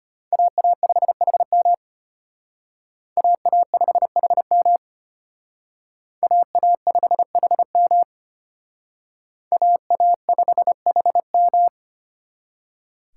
Morse code, heard as a keyed tone, about 30 wpm, 700 Hz. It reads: AA5HM AA5HM AA5HM AA5HM
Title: Speed Racing - USA 2x2 Call Signs 25wpm